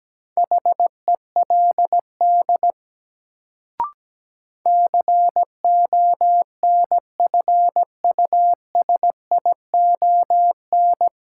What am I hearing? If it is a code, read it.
Morse code, heard as a keyed tone, about 17 wpm, 700 Hz. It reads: HELD CONFUSION
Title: Sentences from Top 1600 Words - New Word Drill - 17wpm